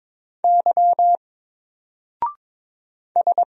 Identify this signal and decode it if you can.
Morse code, heard as a keyed tone, about 22 wpm, 700 Hz. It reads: Y S
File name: Single-Letter-Number-22wpm